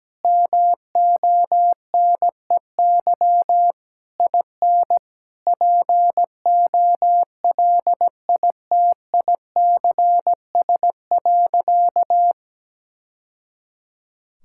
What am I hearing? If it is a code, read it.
Morse code, heard as a keyed tone, about 17 wpm, 700 Hz. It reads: MONEY IN POLITICS.
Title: Sentences from Top 1000 Words - Review Part 6 - 17wpm